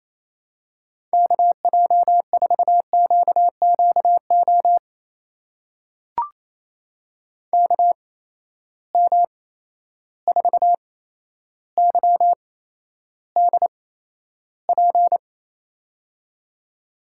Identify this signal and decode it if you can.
Morse code, heard as a keyed tone, about 27 wpm, spaced as for 12 wpm, 700 Hz. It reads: KJ4QQO KM4YDP
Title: USA Call Signs - 6 Characters - 8x Character Spacing - 28wpm